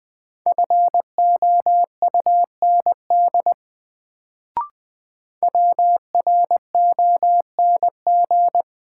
Morse code, heard as a keyed tone, about 21 wpm, 700 Hz. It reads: FOUND WRONG
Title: Sets of 5 Words Encore - New Word Drill - 20wpm